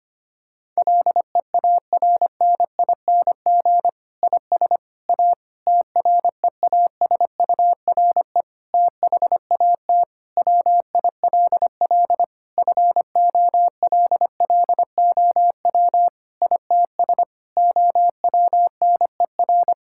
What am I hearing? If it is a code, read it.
Morse code, heard as a keyed tone, about 25 wpm, 700 Hz. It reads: LEARNING IS A TREASURE THAT WILL FOLLOW ITS OWNER
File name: practice / 25wpm / Difficult Sentences with Hints - 25wpm